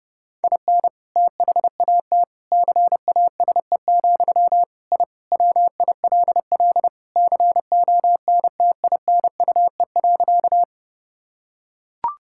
Morse code, heard as a keyed tone, about 30 wpm, 700 Hz. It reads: IN THAT CASE, I WILL CONTINUE.
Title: Sentences from Top 1000 Words - Review Part 6 - 30wpm